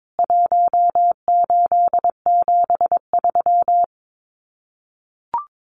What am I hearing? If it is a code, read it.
Morse code, heard as a keyed tone, about 22 wpm, 700 Hz. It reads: 1873
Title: CWT Exchange 22wpm